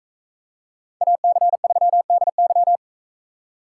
Morse code, heard as a keyed tone, about 44 wpm, 700 Hz. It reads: AC3DY